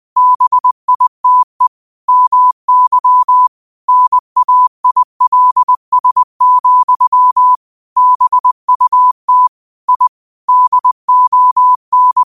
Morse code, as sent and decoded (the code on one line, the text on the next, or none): -... .. - . / -- -.-- / -. .- .. .-.. ... --..-- / -... ..- - / .. / -.. --- -.
BITE MY NAILS, BUT I DON